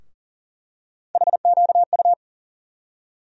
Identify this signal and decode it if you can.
Morse code, heard as a keyed tone, about 40 wpm, 700 Hz. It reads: HQU
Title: USA Call Sign Postfixes - 3 Letters 40wpm